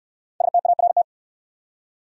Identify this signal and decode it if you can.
Morse code, heard as a keyed tone, about 68 wpm, 700 Hz. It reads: START